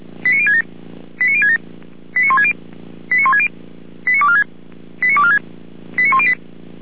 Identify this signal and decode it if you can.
Five-tone selective call (ZVEI), heard as quick runs of five tones, tones of about 70 ms. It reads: 89067 89067 89160 89160 89246 89246 89108